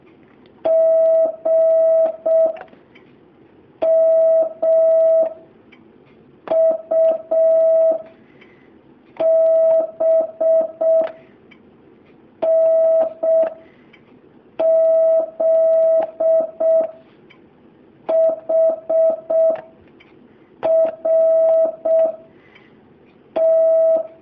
Morse code, heard as a keyed tone, about 6 wpm, 650 Hz. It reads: G M U B N Z H R T
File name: jh8cbh / 2019cw